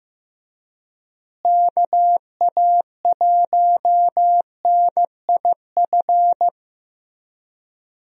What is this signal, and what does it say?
Morse code, heard as a keyed tone, about 15 wpm, 700 Hz. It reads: KA1NIF
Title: USA Call Signs - 6 Characters - Repeated 3x - 15wpm